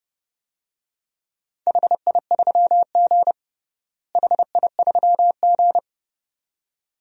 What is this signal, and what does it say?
Morse code, heard as a keyed tone, about 29 wpm, 700 Hz. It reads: HI3G HI3G
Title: DXCC-100 Call Signs - 4 Characters - Repeated 2x - 30wpm